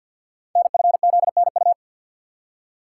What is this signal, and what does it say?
Morse code, heard as a keyed tone, about 50 wpm, 700 Hz. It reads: NFZNU
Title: Code Groups - 5 Letters - 50wpm